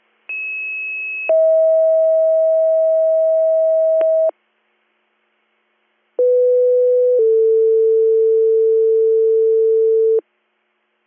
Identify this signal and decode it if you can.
Two-tone page, heard as a one-second tone, then a three-second tone, about 2600 then 640 Hz, about 500 then 450 Hz. It is heard from 0.3 to 10.2 s.